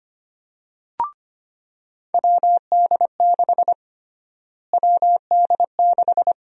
Morse code, heard as a keyed tone, about 25 wpm, 700 Hz. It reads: WD6 WD6
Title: USA Call Sign Prefixes - Repeated 2x - 25wpm